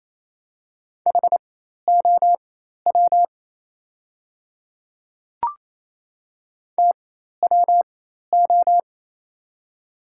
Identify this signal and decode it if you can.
Morse code, heard as a keyed tone, about 28 wpm, spaced as for 13 wpm, 700 Hz. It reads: HOW TWO
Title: 4x Character Spacing - Top 100 Words - 28wpm